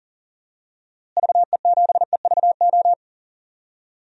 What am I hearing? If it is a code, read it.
Morse code, heard as a keyed tone, about 40 wpm, 700 Hz. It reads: VE7EVO
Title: DXCC-100 Call Signs - 6 Characters - Repeated 2x - 40wpm